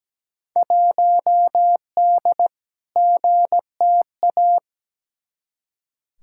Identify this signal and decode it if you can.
Morse code, heard as a keyed tone, about 17 wpm, 700 Hz. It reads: 1D GTA
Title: Field Day Exchanges - 17wpm